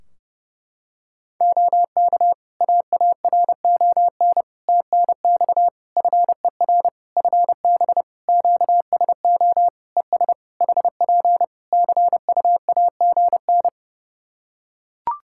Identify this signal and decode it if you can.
Morse code, heard as a keyed tone, about 30 wpm, 700 Hz. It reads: OK AARON TNX FER FB QSO ES HP CUAGN
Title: K1USN SST QSO Practice 30wpm